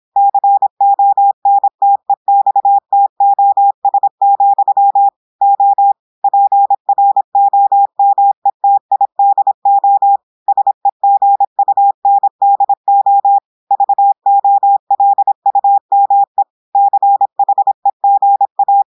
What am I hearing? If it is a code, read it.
Morse code, heard as a keyed tone, about 26 wpm, 800 Hz. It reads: CONTEXTOS, O PROMETIDO SEGUNDO VOLUME CHEGA
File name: audios / portugais / portugais texte en clair 1 vitesse 26 mots minutes